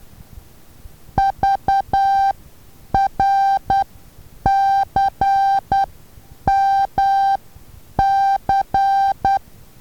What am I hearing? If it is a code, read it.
Morse code, heard as a keyed tone, about 12 wpm, spaced as for 8 wpm, 790 Hz. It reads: VRCMC
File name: Taviro HA6VA B